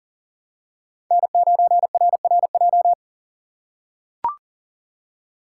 Morse code, heard as a keyed tone, about 40 wpm, 700 Hz. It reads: N9RRJ
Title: USA Call Signs - 6x Character Spacing - 40wpm